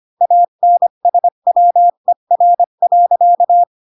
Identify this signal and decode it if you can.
Morse code, heard as a keyed tone, about 25 wpm, 700 Hz. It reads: ANSWER.
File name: Pinocchio / Pinocchio ch3 20 25 ik1hge